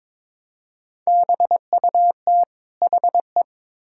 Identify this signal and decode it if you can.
Morse code, heard as a keyed tone, about 22 wpm, 700 Hz. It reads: BUT HE
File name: Build A Phrase - 2-Word Phrases - 22wpm